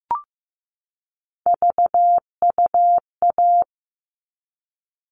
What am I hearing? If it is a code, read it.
Morse code, heard as a keyed tone, about 15 wpm, 700 Hz. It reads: VUA
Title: All Cut Numbers - 3-Digit 15wpm